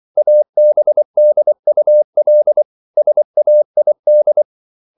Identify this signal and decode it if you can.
Morse code, heard as a keyed tone, about 24 wpm, 600 Hz. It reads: ABDUL SAID